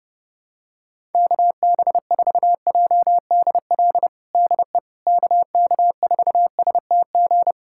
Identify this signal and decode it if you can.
Morse code, heard as a keyed tone, about 30 wpm, 700 Hz. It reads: KB4JDL DE KK4STG
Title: USA Callsign DE Callsign 30wpm